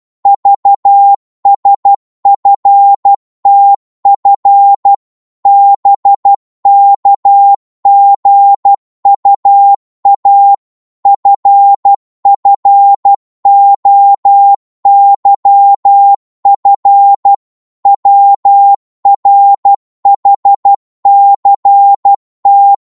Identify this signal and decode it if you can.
Morse code, heard as a keyed tone, about 12 wpm, 800 Hz. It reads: VSFTF BKGUA FFOYF WRHCT
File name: Morse / ltr-12wpm